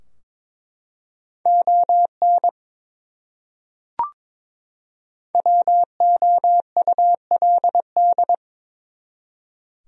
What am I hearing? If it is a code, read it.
Morse code, heard as a keyed tone, about 22 wpm, 700 Hz. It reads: ON WOULD
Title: Top 100 Words 22wpm